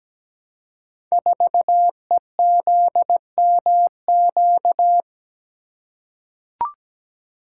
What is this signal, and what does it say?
Morse code, heard as a keyed tone, about 17 wpm, 700 Hz. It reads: 4EZMQ
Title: Code Groups - 5 Letter:Numbers - 17wpm